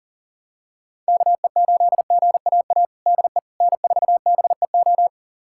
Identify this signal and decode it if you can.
Morse code, heard as a keyed tone, about 40 wpm, 700 Hz. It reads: KE8GAA DE N4BEO
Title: USA Callsign DE Callsign 40wpm